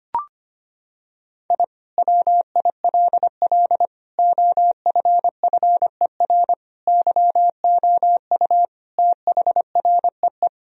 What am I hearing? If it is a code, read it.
Morse code, heard as a keyed tone, about 25 wpm, 700 Hz. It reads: I WILL OFFER YOU THREE
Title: Sentences from Top 1000 Words - Review Part 4 - 25wpm